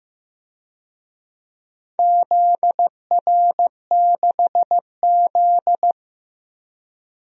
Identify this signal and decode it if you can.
Morse code, heard as a keyed tone, about 16 wpm, 700 Hz. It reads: ZR6Z
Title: DXCC-100 Call Signs - 4 Characters - Repeated 2x - 15wpm